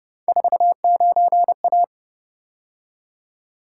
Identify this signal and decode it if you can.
Morse code, heard as a keyed tone, about 30 wpm, 700 Hz. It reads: 49A
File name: Speed-Racing - Three Letter-Number Combinations - 25wpm